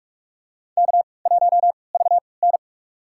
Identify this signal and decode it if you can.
Morse code, heard as a keyed tone, about 45 wpm, spaced as for 25 wpm, 700 Hz. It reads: K1VN
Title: USA Call Signs - 3x Character Spacing - 45wpm